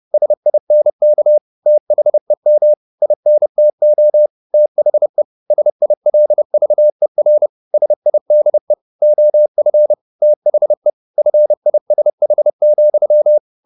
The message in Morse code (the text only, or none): SINK THEM INTO THE SILVER SIDE OF THE FISH,